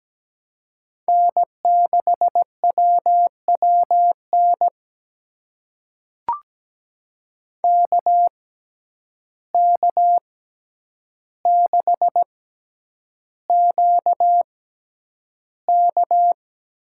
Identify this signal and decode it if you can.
Morse code, heard as a keyed tone, about 17 wpm, spaced as for 9 wpm, 700 Hz. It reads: N6WWN KK6QK
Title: USA Call Signs - 6x Character Spacing - 17wpm